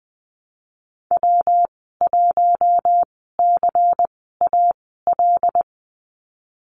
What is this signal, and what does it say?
Morse code, heard as a keyed tone, about 20 wpm, spaced as for 15 wpm, 700 Hz. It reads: W1CAL
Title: 154 Lesson 25g - Letter p - Call Signs 20wpm